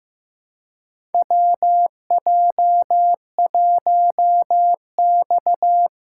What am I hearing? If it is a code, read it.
Morse code, heard as a keyed tone, about 15 wpm, 700 Hz. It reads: WJ1X
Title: USA Call Signs - 4 Characters - 2x Character Spacing - 15wpm